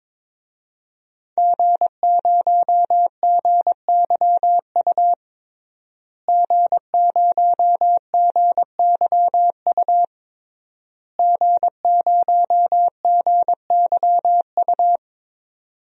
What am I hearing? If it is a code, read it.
Morse code, heard as a keyed tone, about 22 wpm, 700 Hz. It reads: G0GYU G0GYU G0GYU
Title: DXCC-100 Call Signs - 5 Characters - Repeated 3x - 22wpm